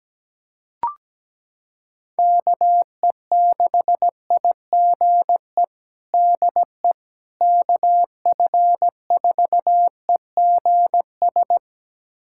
Morse code, heard as a keyed tone, about 17 wpm, 700 Hz. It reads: KE6IGE DE KF4EGS
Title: USA Callsign DE Callsign - 6 Characters - 17wpm